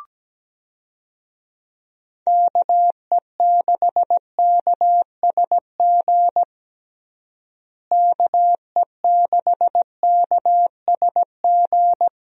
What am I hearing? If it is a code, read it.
Morse code, heard as a keyed tone, about 17 wpm, 700 Hz. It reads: KE6KSG KE6KSG
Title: USA Call Signs - 6 Characters - Repeated 2x - 17wpm